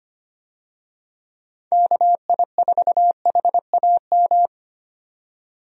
Morse code, heard as a keyed tone, about 25 wpm, 700 Hz. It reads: KI4HAM
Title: USA Call Signs - 6 Characters - 25wpm